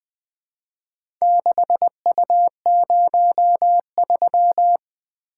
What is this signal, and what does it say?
Morse code, heard as a keyed tone, about 20 wpm, 700 Hz. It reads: 6U03
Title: Four Letter-Number Combinations - 20wpm